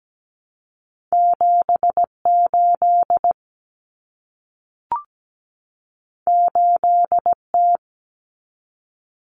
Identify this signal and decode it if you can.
Morse code, heard as a keyed tone, about 17 wpm, 700 Hz. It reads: 78 8T
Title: Common Cut Numbers - 2-Digit 17wpm